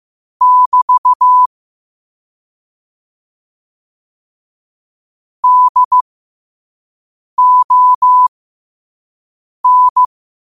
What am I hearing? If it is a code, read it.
Morse code, heard as a keyed tone, about 15 wpm, spaced as for 5 wpm, 1000 Hz. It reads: <BT> DON
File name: qotdfiles / 5wpm